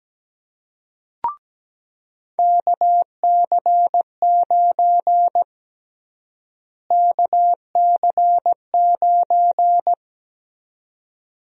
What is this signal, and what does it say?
Morse code, heard as a keyed tone, about 17 wpm, 700 Hz. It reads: KC9 KC9
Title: USA Call Sign Prefixes - Repeated 2x - 17wpm